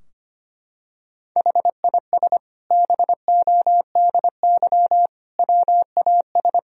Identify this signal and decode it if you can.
Morse code, heard as a keyed tone, about 25 wpm, 700 Hz. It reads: HIS BODY WAS
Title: Sentences from Top 1000 Words - Review Part 2 - 25wpm